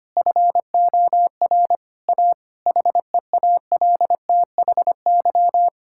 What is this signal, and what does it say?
Morse code, heard as a keyed tone, about 25 wpm, 700 Hz. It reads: FOR A HEALTHY
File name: practice / 25wpm / Difficult Sentences with Hints - 25wpm